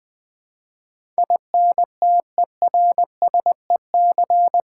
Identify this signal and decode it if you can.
Morse code, heard as a keyed tone, about 21 wpm, 700 Hz. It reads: INTERSEC
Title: Build A Word - 4 Letters At A Time - 12-Letter Words - 20wpm